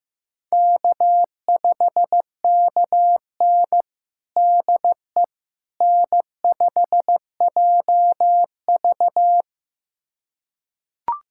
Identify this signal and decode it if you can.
Morse code, heard as a keyed tone, about 15 wpm, 700 Hz. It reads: K5KN DE N5JV
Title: USA Callsign DE Callsign - 4 Characters - 15wpm